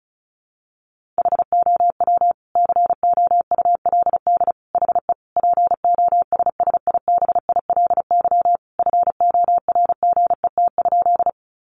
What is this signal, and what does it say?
Morse code, heard as a keyed tone, about 34 wpm, 700 Hz. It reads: HOW COULD HE POSSIBILY FORGET?
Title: Sentences from Top 1300 Words 35wpm